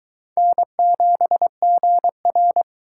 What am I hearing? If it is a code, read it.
Morse code, heard as a keyed tone, about 23 wpm, 700 Hz. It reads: N7GR